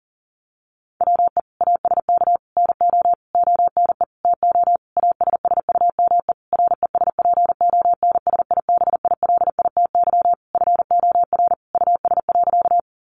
Morse code, heard as a keyed tone, about 40 wpm, 700 Hz. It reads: WE ASK NO ONE TO ASSUME RESPONSIBILITY FOR US.
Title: Sentences from Top 1500 Words 40wpm